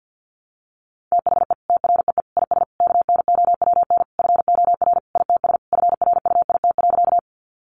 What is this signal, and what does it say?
Morse code, heard as a keyed tone, about 48 wpm, 700 Hz. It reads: THE TREE IS KNOWN FOR ITS FRUIT.
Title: Sentences from Top 1300 Words 50wpm